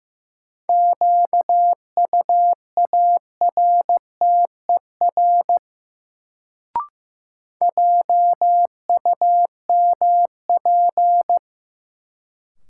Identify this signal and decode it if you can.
Morse code, heard as a keyed tone, about 15 wpm, 700 Hz. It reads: QUARTER JUMP
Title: Sentences from Top 1300 Words - New Word Drill Alt - 15wpm